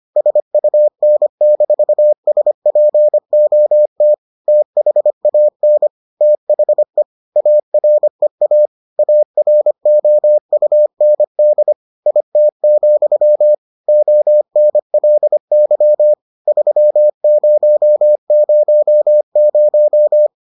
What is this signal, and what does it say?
Morse code, heard as a keyed tone, about 25 wpm, 600 Hz. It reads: SUN-SPOT THAN THE AREA AROUND IT, ONLY 3000